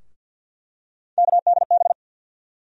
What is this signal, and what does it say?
Morse code, heard as a keyed tone, about 48 wpm, 700 Hz. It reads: KDB
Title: USA Call Sign Postfixes - 3 Letters 50wpm